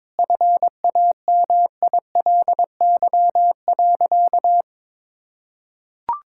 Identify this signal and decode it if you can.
Morse code, heard as a keyed tone, about 22 wpm, 700 Hz. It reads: FAMILY.
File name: Difficult Sentences with Hints - 22wpm